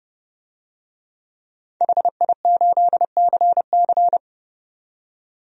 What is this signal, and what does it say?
Morse code, heard as a keyed tone, about 30 wpm, 700 Hz. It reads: HI8CC
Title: DXCC-100 Call Signs - 5 Characters - 30wpm